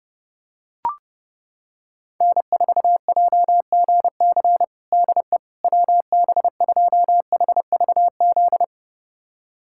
Morse code, heard as a keyed tone, about 32 wpm, 700 Hz. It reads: N4JGC DE WB2HVZ
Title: USA Callsign DE Callsign 30wpm